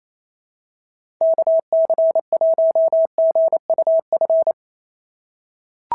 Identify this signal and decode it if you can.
Morse code, heard as a keyed tone, about 27 wpm, 650 Hz. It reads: KC1GUF
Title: SOTA QSO Practice 28wpm